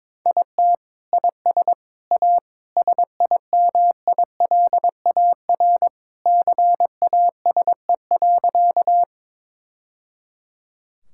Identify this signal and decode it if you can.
Morse code, heard as a keyed tone, about 22 wpm, 700 Hz. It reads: IT IS A SIMILAR CASE.
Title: Sentences from Top 1000 Words - Review Part 5 - 22wpm